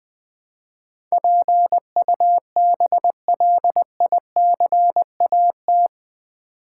Morse code, heard as a keyed tone, about 20 wpm, 700 Hz. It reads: PUBLICAT
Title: Build A Word - 4 Letters At A Time - 11-Letter Words - 20wpm